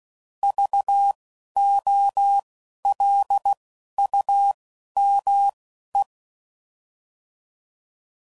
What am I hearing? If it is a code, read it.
Morse code, heard as a keyed tone, about 16 wpm, spaced as for 11 wpm, 800 Hz. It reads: VOLUME
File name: Prefixes / By First Letter / Vs